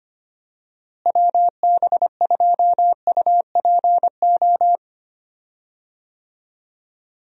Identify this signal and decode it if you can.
Morse code, heard as a keyed tone, about 25 wpm, 700 Hz. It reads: WB2UPO
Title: USA Call Signs - 6 Characters - Repeated 3x - 25wpm